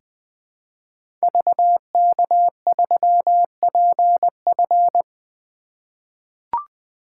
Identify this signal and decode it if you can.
Morse code, heard as a keyed tone, about 21 wpm, 700 Hz. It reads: VK3PF
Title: 2023 SOTA Activator Call Signs - 20wpm